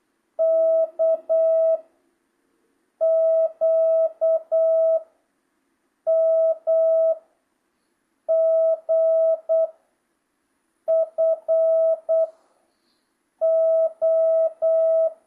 Morse code, heard as a keyed tone, about 8 wpm, 650 Hz. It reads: K Q M G F O
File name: jh8cbh / 2025cw